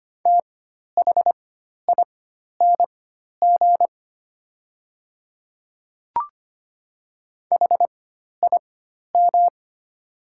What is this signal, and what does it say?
Morse code, heard as a keyed tone, about 25 wpm, spaced as for 12 wpm, 700 Hz. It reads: THING HIM
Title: 4x Character Spacing - Top 300 Words - 25wpm